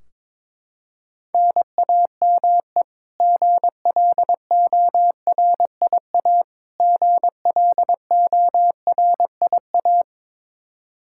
Morse code, heard as a keyed tone, about 21 wpm, 700 Hz. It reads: NAME GLORIA GLORIA
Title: K1USN SST QSO Practice 22wpm